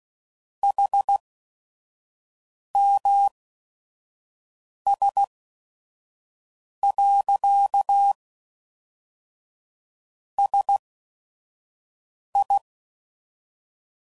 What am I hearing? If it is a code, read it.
Morse code, heard as a keyed tone, about 16 wpm, spaced as for 5 wpm, 800 Hz. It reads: HMS. SI
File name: CW Code Groups @5WPM / Technical Subjects / XC